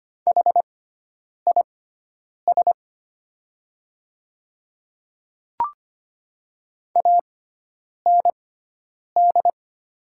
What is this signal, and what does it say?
Morse code, heard as a keyed tone, about 25 wpm, spaced as for 9 wpm, 690 Hz. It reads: HIS AND
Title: 6x Character Spacing - Top 100 Words - 25wpm